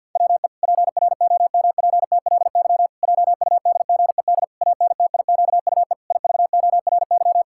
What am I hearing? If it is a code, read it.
Morse code, heard as a keyed tone, about 48 wpm, 700 Hz. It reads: WE PROMPTLY JUDGED ANTIQUE IVORY
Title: Pangram Challenge v2 - 50wpm